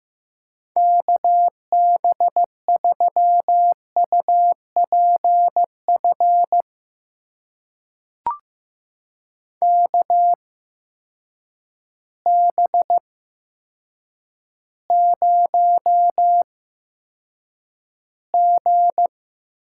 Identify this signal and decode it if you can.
Morse code, heard as a keyed tone, about 15 wpm, spaced as for 7 wpm, 700 Hz. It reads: KB3UPF KB0G
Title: USA Call Signs - 6 Characters - 8x Character Spacing - 15wpm